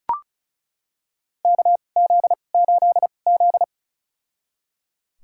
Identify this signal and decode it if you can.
Morse code, heard as a keyed tone, about 36 wpm, spaced as for 25 wpm, 700 Hz. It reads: KZ8Z
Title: USA Call Signs - 4 Characters - 2x Character Spacing - 35wpm